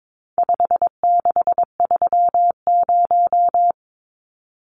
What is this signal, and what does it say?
Morse code, heard as a keyed tone, about 22 wpm, 700 Hz. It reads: 5630
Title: Numbers - 4-Digit 22wpm